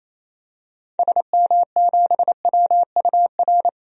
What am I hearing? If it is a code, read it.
Morse code, heard as a keyed tone, about 28 wpm, 700 Hz. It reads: SM7WUR
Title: DXCC-100 Call Signs - 6 Characters - 28wpm